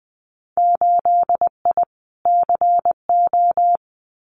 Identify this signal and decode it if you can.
Morse code, heard as a keyed tone, about 20 wpm, 700 Hz. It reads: 8I CO
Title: Winter Field Day Exchanges Encore 20wpm